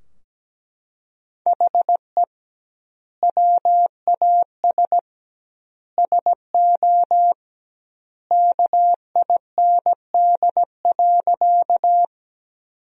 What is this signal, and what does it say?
Morse code, heard as a keyed tone, about 17 wpm, 700 Hz. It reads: HE WAS SO KIND.
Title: 2x Word-Spacing - Sentences from Top 200 Words - 17wpm